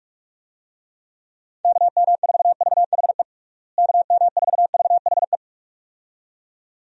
Morse code, heard as a keyed tone, about 46 wpm, 700 Hz. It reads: KM4VHE KM4VHE
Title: USA Call Signs - 6 Characters - Repeated 2x - 45wpm